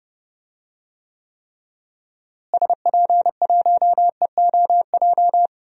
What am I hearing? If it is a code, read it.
Morse code, heard as a keyed tone, about 31 wpm, 700 Hz. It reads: SP1EOJ
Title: DXCC-100 Call Signs - 6 Characters - Repeated 3x - 30wpm